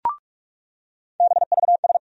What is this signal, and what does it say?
Morse code, heard as a keyed tone, about 42 wpm, 700 Hz. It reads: BVS
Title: USA Call Sign Postfixes - 3 Letters 45wpm